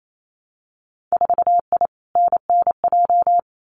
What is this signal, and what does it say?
Morse code, heard as a keyed tone, about 28 wpm, 700 Hz. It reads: 4I NNJ
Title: Winter Field Day Exchanges 28wpm